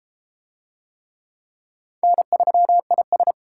Morse code, heard as a keyed tone, about 29 wpm, 700 Hz. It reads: N3IS